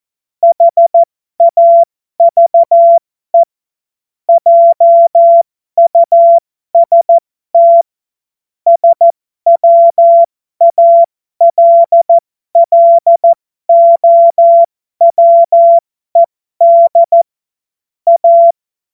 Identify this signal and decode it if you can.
Morse code, heard as a keyed tone, about 14 wpm, spaced as for 12 wpm, 680 Hz. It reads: HAVE JUST SWALLOWED A
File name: Morsekurs / Witze